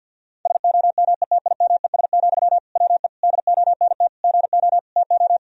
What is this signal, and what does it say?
Morse code, heard as a keyed tone, about 50 wpm, 700 Hz. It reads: SOMETIMES, WE DONT GO TO
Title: Sentences from Top 1000 Words - Review Part 3 - 50wpm